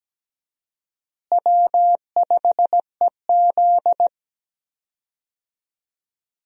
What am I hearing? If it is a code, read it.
Morse code, heard as a keyed tone, about 17 wpm, 700 Hz. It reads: W5EZ
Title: USA Call Signs - 4 Characters - Repeated 3x - 17wpm